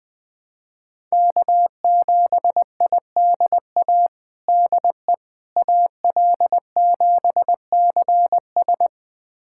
Morse code, heard as a keyed tone, about 20 wpm, 700 Hz. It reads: K7IDA DE AL7CS
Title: USA Callsign DE Callsign - 5 Characters - 20wpm